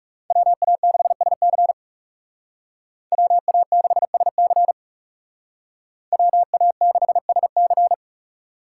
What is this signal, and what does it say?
Morse code, heard as a keyed tone, about 35 wpm, 700 Hz. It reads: WA6SC WA6SC WA6SC
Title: Speed Racing - USA 2x2 Call Signs 30wpm